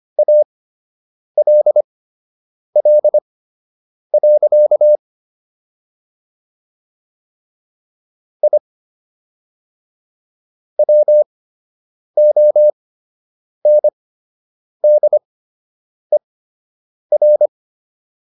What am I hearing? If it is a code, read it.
Morse code, heard as a keyed tone, about 25 wpm, spaced as for 8 wpm, 600 Hz. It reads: ALL. I WONDER